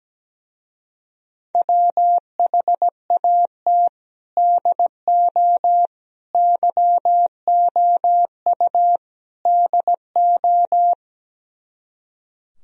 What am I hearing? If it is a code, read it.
Morse code, heard as a keyed tone, about 17 wpm, 700 Hz. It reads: WHAT DO YOU DO
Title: Speed-Racing - Sets of 4 Words 17wpm